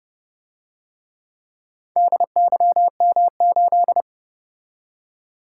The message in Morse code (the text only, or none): DYM8